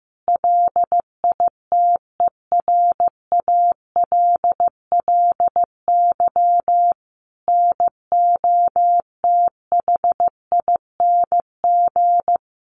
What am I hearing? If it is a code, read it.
Morse code, heard as a keyed tone, about 15 wpm, 700 Hz. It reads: LITERALLY NOTHING
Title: Sentences from Top 1400 Words 15wpm